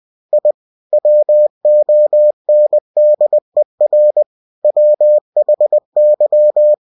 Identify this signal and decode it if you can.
Morse code, heard as a keyed tone, about 20 wpm, 600 Hz. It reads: I WONDER WHY
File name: cw-books / winnie-the-pooh / winnie-20wpm-page-0003